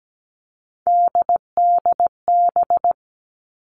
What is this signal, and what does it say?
Morse code, heard as a keyed tone, about 17 wpm, 700 Hz. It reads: DDB
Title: All Cut Numbers - 3-Digit 17wpm